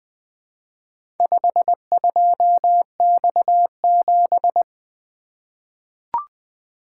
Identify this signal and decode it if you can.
Morse code, heard as a keyed tone, about 20 wpm, 700 Hz. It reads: 52X7
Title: Four Letter-Number Combinations - 20wpm